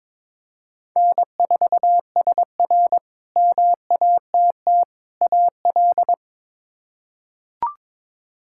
Morse code, heard as a keyed tone, about 22 wpm, 700 Hz. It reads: N4SR MATT AL
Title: K1USN SST QSO Practice - 22wpm